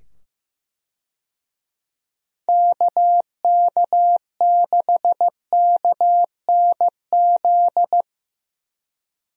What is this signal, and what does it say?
Morse code, heard as a keyed tone, about 16 wpm, 700 Hz. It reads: KK6KNZ
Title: USA Call Signs - 6 Characters - 15wpm